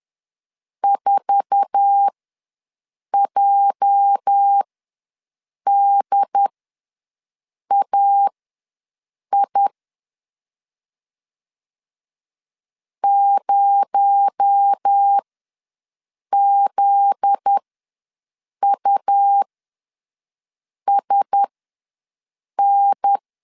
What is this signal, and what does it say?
Morse code, heard as a keyed tone, about 10 wpm, spaced as for 6 wpm, 800 Hz. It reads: 4JDAI 0ZUSN
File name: LEZIONI / 8 YJ45 / 8 YJ4 ws 10wpm 4020